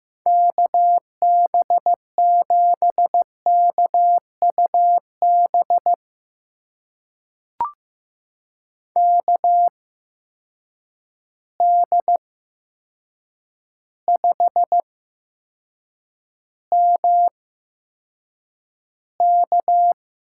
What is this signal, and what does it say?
Morse code, heard as a keyed tone, about 15 wpm, spaced as for 7 wpm, 700 Hz. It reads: KB7KUB KD5MK
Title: USA Call Signs - 6 Characters - 8x Character Spacing - 15wpm